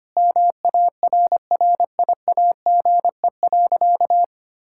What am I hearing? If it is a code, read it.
Morse code, heard as a keyed tone, about 25 wpm, 700 Hz. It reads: MARRIAGE.
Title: Sentences from Top 1000 Words - Review Part 6 - 25wpm